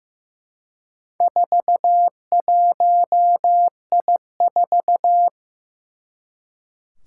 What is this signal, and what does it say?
Morse code, heard as a keyed tone, about 15 wpm, 700 Hz. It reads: 41I4
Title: Four Challenging Letter-Number Combinations 15wpm